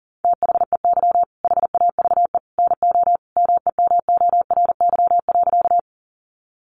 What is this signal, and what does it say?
Morse code, heard as a keyed tone, about 40 wpm, 700 Hz. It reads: THEY HAVE NO MEMORY.
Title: Sentences from Top 1300 Words 40wpm